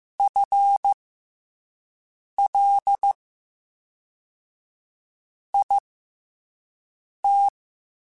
Morse code, heard as a keyed tone, about 15 wpm, spaced as for 4 wpm, 800 Hz. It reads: FLIT
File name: FL / FL mp3s / FL Prefixes 3x2 sp @5